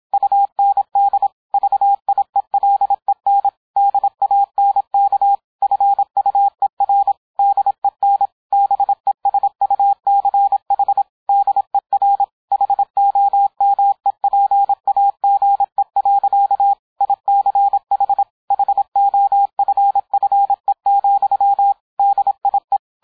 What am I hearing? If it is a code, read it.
Morse code, heard as a keyed tone, about 27 wpm, 800 Hz, with radio noise. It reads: UND VIELEN DANK FUER DEN BESUCH DER HOMEPAGE. ICH HOFFE, DIE